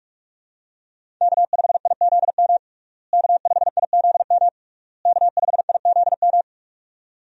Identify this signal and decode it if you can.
Morse code, heard as a keyed tone, about 42 wpm, 700 Hz. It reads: K5IZM K5IZM K5IZM
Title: USA Call Signs - 5 Characters - Repeated 3x - 45wpm